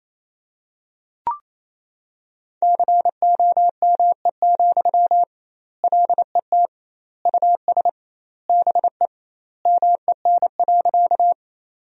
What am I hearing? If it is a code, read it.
Morse code, heard as a keyed tone, about 27 wpm, 700 Hz. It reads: COME, LET US BE MEN.
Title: 2x Word-Spacing - Sentences from Top 300 Words - 28wpm